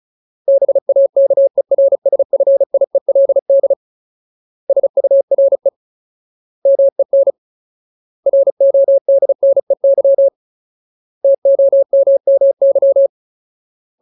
Morse code, heard as a keyed tone, about 34 wpm, 550 Hz. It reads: BAKERSFIELD SURE MEN RODNEY TOMMY
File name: Ultimate Word List / 35WPM Ultimate